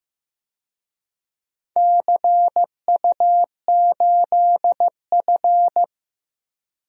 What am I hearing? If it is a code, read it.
Morse code, heard as a keyed tone, about 15 wpm, 700 Hz. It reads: CU8F
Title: DXCC-100 Call Signs - 4 Characters - Repeated 2x - 15wpm